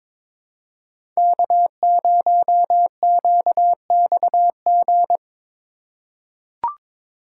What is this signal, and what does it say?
Morse code, heard as a keyed tone, about 22 wpm, 700 Hz. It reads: K0QXG
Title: USA Call Signs - 5 Characters - 3x Character Spacing - 22wpm